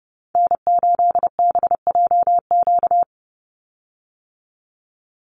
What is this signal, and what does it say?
Morse code, heard as a keyed tone, about 30 wpm, 700 Hz. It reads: N8BJQ
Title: Call Signs - 30wpm